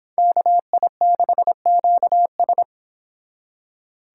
Morse code, heard as a keyed tone, about 29 wpm, 700 Hz. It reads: KI6QS